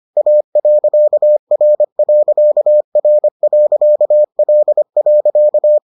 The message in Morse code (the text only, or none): A.R.R.L.